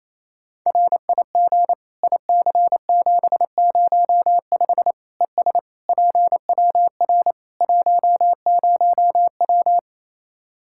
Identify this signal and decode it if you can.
Morse code, heard as a keyed tone, about 27 wpm, 700 Hz. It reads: RIG IC705 ES PWR 10W
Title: Practice QSO - Exchange 2 - Info - 28wpm